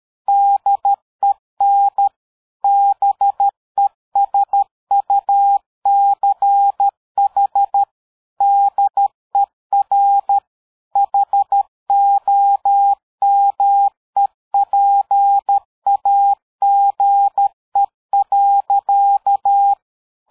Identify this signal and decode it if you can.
Morse code, heard as a keyed tone, about 13 wpm, 800 Hz. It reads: DEN BESUCH DER HOMEPAGE.
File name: dk5ke / PersWorte 60